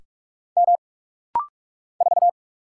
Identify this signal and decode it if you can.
Morse code, heard as a keyed tone, about 46 wpm, 700 Hz. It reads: M 4
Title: Single Letter-Number Alt - Rapid-Fire 45wpm